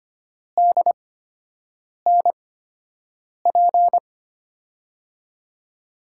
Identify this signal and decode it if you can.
Morse code, heard as a keyed tone, about 25 wpm, spaced as for 7 wpm, 700 Hz. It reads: DNP
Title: USA Call Signs - 6 Characters - 8x Character Spacing - 25wpm